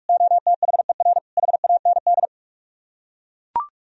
Morse code, heard as a keyed tone, about 44 wpm, 690 Hz. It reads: OTHER HAND